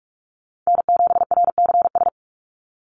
Morse code, heard as a keyed tone, about 44 wpm, 700 Hz. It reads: N7RCS
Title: Call Signs Repeated 3x - 45wpm